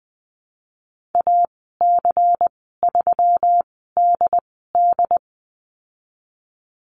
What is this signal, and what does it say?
Morse code, heard as a keyed tone, about 20 wpm, 700 Hz. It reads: A C 3 D D
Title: 154 Lesson 25g - Letter p - Call Signs 20wpm